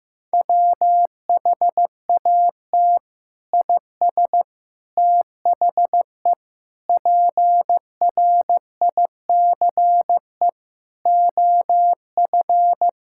Morse code, heard as a keyed tone, about 14 wpm, 700 Hz. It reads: WHAT IS THE PRICE OF
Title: Sentences from Top 1000 Words - Review Part 5 - 15wpm